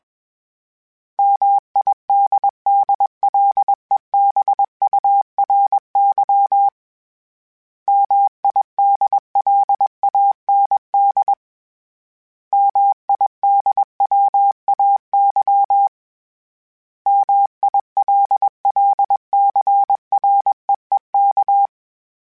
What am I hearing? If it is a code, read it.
Morse code, heard as a keyed tone, about 21 wpm, 800 Hz. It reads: MIDDLEBURY MIDLAND MIDWAY MILLCREEK
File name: fromSMs / FromSMs mp3s / Cities@20